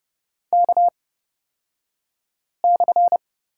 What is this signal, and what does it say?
Morse code, heard as a keyed tone, about 31 wpm, spaced as for 5 wpm, 700 Hz. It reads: K/
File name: Lesson 36d - BT - Two Characters 30wpm